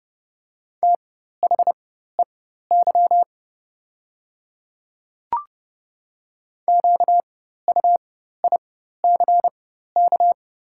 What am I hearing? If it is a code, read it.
Morse code, heard as a keyed tone, about 30 wpm, spaced as for 14 wpm, 690 Hz. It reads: THEY QUICK
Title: 4x Character Spacing - Top 500 Words - 30wpm